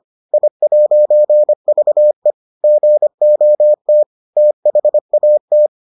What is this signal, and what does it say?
Morse code, heard as a keyed tone, about 25 wpm, 600 Hz. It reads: I'VE GOT THAT